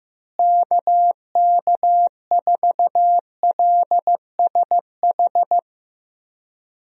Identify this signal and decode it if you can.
Morse code, heard as a keyed tone, about 15 wpm, 700 Hz. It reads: KK4LSH
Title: USA Call Signs - 6 Characters - Repeated 3x - 15wpm